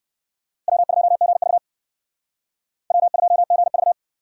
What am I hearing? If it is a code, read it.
Morse code, heard as a keyed tone, about 59 wpm, 700 Hz. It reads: W2GV W2GV